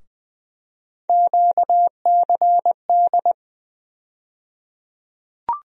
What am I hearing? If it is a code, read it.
Morse code, heard as a keyed tone, about 20 wpm, 700 Hz. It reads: QCD
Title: USA Call Sign Postfixes - 3 Letters 20wpm